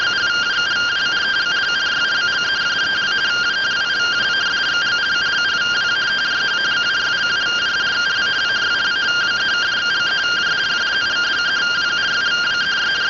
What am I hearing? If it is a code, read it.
Two-tone teleprinter signal, about 170 Hz shift.